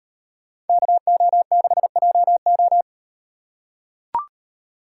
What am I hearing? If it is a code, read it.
Morse code, heard as a keyed tone, about 38 wpm, 700 Hz. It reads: KO6JO